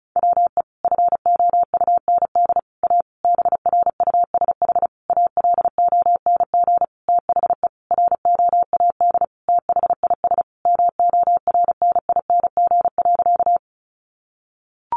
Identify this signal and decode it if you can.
Morse code, heard as a keyed tone, about 34 wpm, 700 Hz. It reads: WE FOUND A BRUSH ALONG THE ROAD THIS MORNING.
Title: Sentences from Top 1100 Words 35wpm